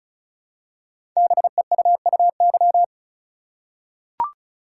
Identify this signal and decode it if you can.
Morse code, heard as a keyed tone, about 36 wpm, 700 Hz. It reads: BEUUY
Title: Code Groups - 5 Letters - 35wpm